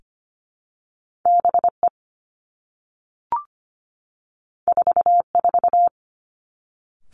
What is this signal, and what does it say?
Morse code, heard as a keyed tone, about 25 wpm, 700 Hz. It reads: B E 4 4
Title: Two Challenging Letter-Number Combinations 25wpm